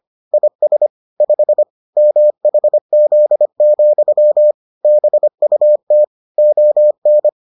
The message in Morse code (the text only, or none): IS 5 MHZ, BUT ON